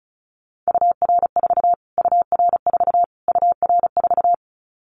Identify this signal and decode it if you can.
Morse code, heard as a keyed tone, about 34 wpm, 700 Hz. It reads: UR4 UR4 UR4
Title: Callsign Prefixes for Top 25 DXCC - Repeated 3x - 35wpm